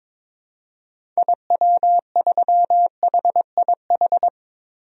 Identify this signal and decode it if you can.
Morse code, heard as a keyed tone, about 22 wpm, 700 Hz. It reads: IW3HIH
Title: DXCC-100 Call Signs - 6 Characters - Repeated 2x - 22wpm